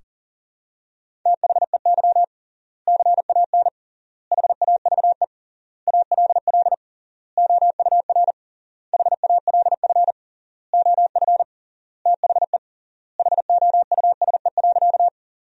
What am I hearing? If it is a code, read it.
Morse code, heard as a keyed tone, about 40 wpm, 700 Hz. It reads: THEY CAN HAVE ALL OUR HALF OF THE HOUSE.
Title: 3x Word-Spacing - Sentences from Top 400 Words - 40wpm